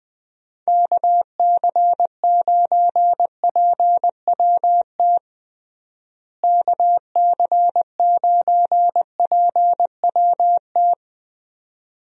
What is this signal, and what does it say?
Morse code, heard as a keyed tone, about 20 wpm, 700 Hz. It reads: KC9PWT KC9PWT
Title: USA Call Signs - 6 Characters - Repeated 2x - 20wpm